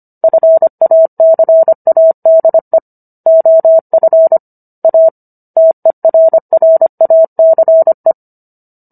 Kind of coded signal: Morse code